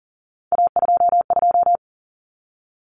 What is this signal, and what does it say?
Morse code, heard as a keyed tone, about 40 wpm, 700 Hz. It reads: A22
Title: Common Cut Numbers - 3-Digit 40wpm